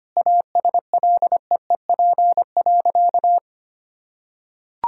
Morse code, heard as a keyed tone, about 25 wpm, 700 Hz. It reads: ASLEEP.
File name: practice / 25wpm / Sentences with 3-Word Phrases - 25wpm